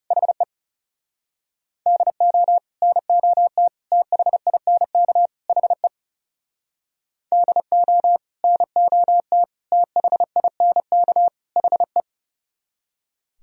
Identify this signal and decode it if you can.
Morse code, heard as a keyed tone, about 33 wpm, 700 Hz. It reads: HE DO NOT THINK HE DO NOT THINK HE
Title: Speed-Racing - Sets of 4 Words 30wpm